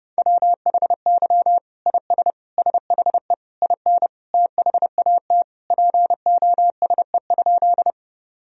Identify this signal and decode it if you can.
Morse code, heard as a keyed tone, about 30 wpm, 700 Hz. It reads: WHY IS SHE IN THAT POSE?
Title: Sentences from Top 1000 Words - Review Part 2 - 30wpm